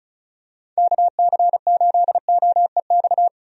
Morse code, heard as a keyed tone, about 36 wpm, 700 Hz. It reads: KC8OEX
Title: USA Call Signs - 6 Characters - 35wpm